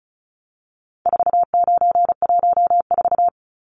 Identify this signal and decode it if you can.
Morse code, heard as a keyed tone, about 34 wpm, 700 Hz. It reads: 4914
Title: Numbers - 4-Digit 35wpm